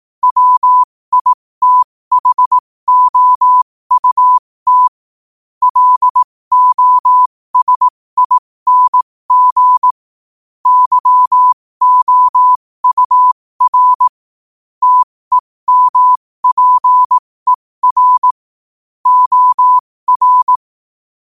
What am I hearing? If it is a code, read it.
Morse code, heard as a keyed tone, about 18 wpm, spaced as for 15 wpm, 1000 Hz. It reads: WITHOUT LOSING YOUR TEMPER OR
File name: 15wpm